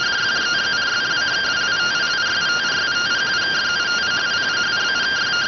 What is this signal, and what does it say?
Two-tone teleprinter signal, about 170 Hz shift.